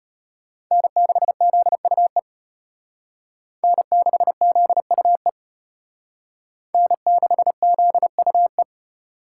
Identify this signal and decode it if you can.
Morse code, heard as a keyed tone, about 32 wpm, 700 Hz. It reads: N6ZUE N6ZUE N6ZUE